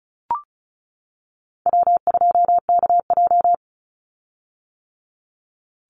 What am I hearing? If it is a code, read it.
Morse code, heard as a keyed tone, about 34 wpm, 700 Hz. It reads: W2KJ
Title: Call Signs - 35wpm